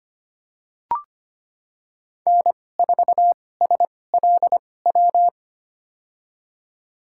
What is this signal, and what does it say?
Morse code, heard as a keyed tone, about 25 wpm, spaced as for 18 wpm, 700 Hz. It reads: N4SLW
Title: USA Call Signs - 5 Characters - 2x Character Spacing - 25wpm